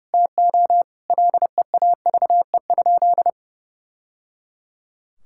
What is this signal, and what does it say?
Morse code, heard as a keyed tone, about 30 wpm, 700 Hz. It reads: TO LEAVE?
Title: Sentences from Top 1000 Words - Review Part 5 - 30wpm